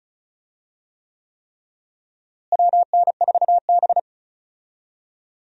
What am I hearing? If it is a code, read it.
Morse code, heard as a keyed tone, about 36 wpm, 700 Hz. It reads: WN4B
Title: USA Call Signs - 4 Characters - Repeated 3x - 35wpm